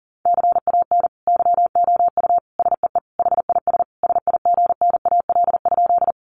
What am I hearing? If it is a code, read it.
Morse code, heard as a keyed tone, about 40 wpm, 700 Hz. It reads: CAN YOU SEE HIS SIGNAL?
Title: Sentences from Top 1400 Words 40wpm